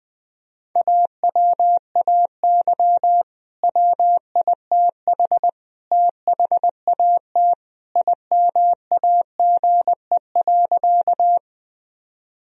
Morse code, heard as a keyed tone, about 20 wpm, 700 Hz. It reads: AWAY WITH THAT IMAGE.
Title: Sentences from Top 1000 Words - Review Part 4 - 20wpm